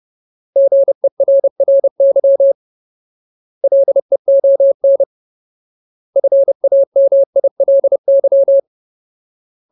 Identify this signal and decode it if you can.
Morse code, heard as a keyed tone, about 30 wpm, 550 Hz. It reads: GERRY LEON FAMILY
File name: Ultimate Word List / 30WPM Ultimate